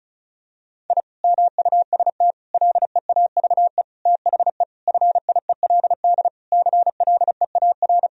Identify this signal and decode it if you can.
Morse code, heard as a keyed tone, about 36 wpm, 700 Hz. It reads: I MUST LEAVE THE FIELD CLEAR
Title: Sentences from Top 1000 Words - Review Part 2 - 35wpm